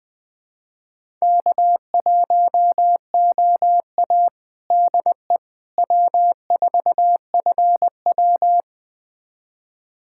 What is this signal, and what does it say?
Morse code, heard as a keyed tone, about 20 wpm, 700 Hz. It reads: K1OA DE W4FW
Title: USA Callsign DE Callsign - 4 Characters - 20wpm